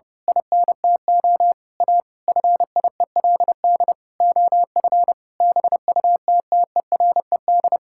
Morse code, heard as a keyed tone, about 31 wpm, 700 Hz. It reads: INTO A FIELD OF BUTTERED
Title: Pangram Challenge v2 - 30wpm